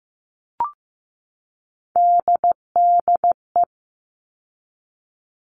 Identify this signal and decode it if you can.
Morse code, heard as a keyed tone, about 15 wpm, 700 Hz. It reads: DDE
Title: All Cut Numbers - 3-Digit 15wpm